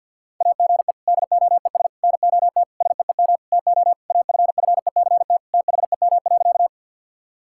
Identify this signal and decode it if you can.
Morse code, heard as a keyed tone, about 50 wpm, 700 Hz. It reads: AGE DOES NOT SEEM TO AFFECT THEM.
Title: Sentences from Top 1000 Words - Review Part 6 - 50wpm